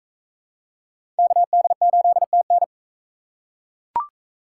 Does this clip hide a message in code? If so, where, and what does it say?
Yes: Morse code from 1.2 to 2.6 s, heard as a keyed tone, about 41 wpm, 700 Hz. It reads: KD8TN